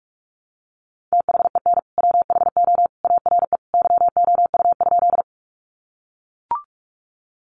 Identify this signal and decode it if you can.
Morse code, heard as a keyed tone, about 44 wpm, 700 Hz. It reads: THEN WHO ARE YOU?
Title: Sentences from Top 100 Words 45wpm